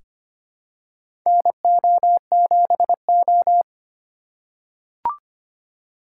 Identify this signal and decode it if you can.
Morse code, heard as a keyed tone, about 25 wpm, 700 Hz. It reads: NO7O
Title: USA Call Signs - 4 Characters - 2x Character Spacing - 25wpm